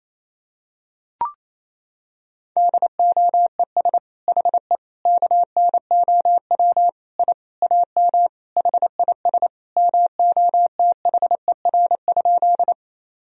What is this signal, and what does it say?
Morse code, heard as a keyed tone, about 28 wpm, 700 Hz. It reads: DOES HE KNOW I AM HIS MOTHER?
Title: Sentences from Top 1000 Words - Review Part 2 - 28wpm